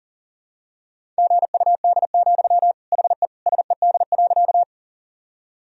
Morse code, heard as a keyed tone, about 40 wpm, 700 Hz. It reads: GUD, HE SED.
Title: Sentences from Top 200 Words and Abbreviations 40wpm